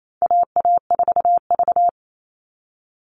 Morse code, heard as a keyed tone, about 28 wpm, 700 Hz. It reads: AA4V
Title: All Cut Numbers - 4-Digit 28wpm